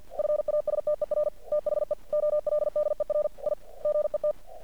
Morse code, heard as a keyed tone, about 50 wpm, 600 Hz. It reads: FASTER THE OLDER I GET
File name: archived / CRSnet